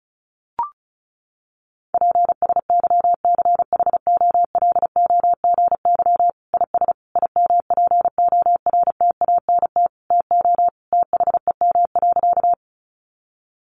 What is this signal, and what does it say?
Morse code, heard as a keyed tone, about 34 wpm, 700 Hz. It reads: PSYCHOLOGY IS IMPORTANT TO THEM.
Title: Sentences from Top 1400 Words 35wpm